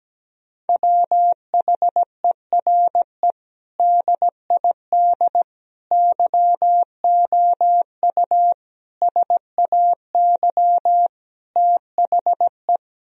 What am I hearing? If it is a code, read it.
Morse code, heard as a keyed tone, about 17 wpm, 700 Hz. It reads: WHERE DID YOU SAY THE
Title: Sentences from Top 1000 Words - Review Part 3 - 17wpm